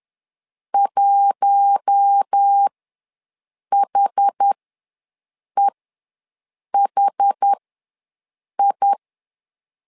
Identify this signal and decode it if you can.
Morse code, heard as a keyed tone, about 10 wpm, spaced as for 6 wpm, 800 Hz. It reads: 1HEHI
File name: LEZIONI / 4 Q1 ws 10wpm 2538